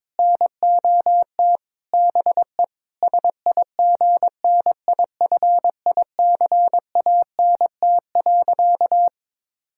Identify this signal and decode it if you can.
Morse code, heard as a keyed tone, about 22 wpm, 700 Hz. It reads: NOT BE SIGNIFICANT.
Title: Sentences from Top 1000 Words - Review Part 6 - 22wpm